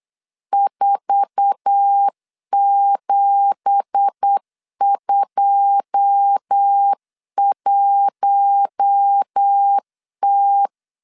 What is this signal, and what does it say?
Morse code, heard as a keyed tone, about 8 wpm, 800 Hz. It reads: 4721T
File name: CW / DET / 1 9 1500C S C 8wpm 5653